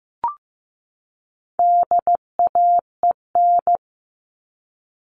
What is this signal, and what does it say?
Morse code, heard as a keyed tone, about 15 wpm, 700 Hz. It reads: DAEN
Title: All Cut Numbers - 4-Digit 15wpm